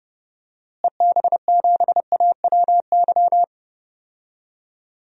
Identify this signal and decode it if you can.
Morse code, heard as a keyed tone, about 30 wpm, 700 Hz. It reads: EB7AWY
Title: DXCC-100 Call Signs - 6 Characters - Repeated 2x - 30wpm